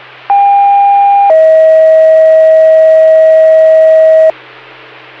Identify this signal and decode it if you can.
Two-tone page, heard as a one-second tone, then a three-second tone, about 800 then 640 Hz. It is heard from 0.3 to 4.3 s.